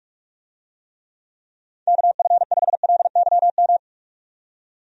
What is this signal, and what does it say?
Morse code, heard as a keyed tone, about 47 wpm, 700 Hz. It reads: KF5LYM
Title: USA Call Signs - 6 Characters - 45wpm